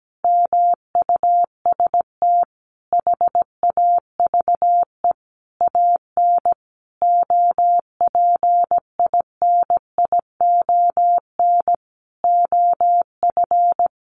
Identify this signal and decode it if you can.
Morse code, heard as a keyed tone, about 17 wpm, 700 Hz. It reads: MUST HAVE AN OPINION OF
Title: Sentences from Top 1300 Words 17wpm